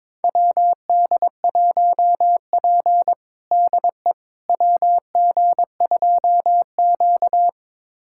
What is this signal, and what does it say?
Morse code, heard as a keyed tone, about 22 wpm, 700 Hz. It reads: WD1P DE WG2Q
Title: USA Callsign DE Callsign - 4 Characters - 22wpm